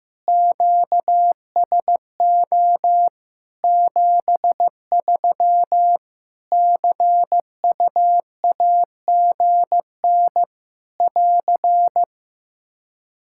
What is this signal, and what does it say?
Morse code, heard as a keyed tone, about 15 wpm, 700 Hz. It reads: QSO 73 CUAGN <AR>
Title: Practice QSO - Exchange 4 - Ending - 15wpm